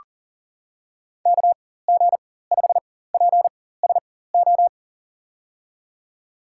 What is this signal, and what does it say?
Morse code, heard as a keyed tone, about 38 wpm, spaced as for 18 wpm, 700 Hz. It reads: KG5PSO
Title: USA Call Signs - 6 Characters - 4x Character Spacing - 40wpm